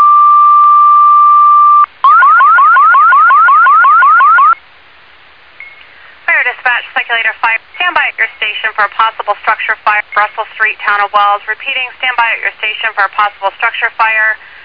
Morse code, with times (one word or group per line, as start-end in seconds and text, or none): none